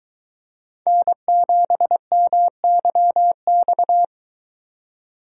Morse code, heard as a keyed tone, about 24 wpm, 700 Hz. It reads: N7MYX